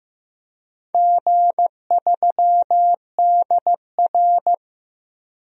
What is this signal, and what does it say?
Morse code, heard as a keyed tone, about 15 wpm, 700 Hz. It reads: G3DR
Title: DXCC-100 Call Signs - 4 Characters - Repeated 3x - 15wpm